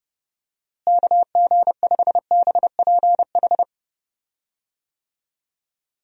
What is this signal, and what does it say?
Morse code, heard as a keyed tone, about 30 wpm, 700 Hz. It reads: KG5BPH
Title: USA Call Signs - 6 Characters - Repeated 3x - 30wpm